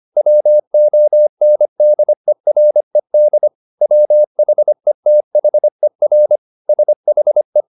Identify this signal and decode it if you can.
Morse code, heard as a keyed tone, about 25 wpm, 600 Hz. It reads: WONDERED WHETHER SHE